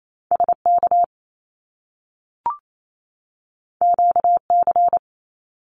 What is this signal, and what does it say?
Morse code, heard as a keyed tone, about 28 wpm, 700 Hz. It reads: SK QC
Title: Canadian Provinces and Territories 28wpm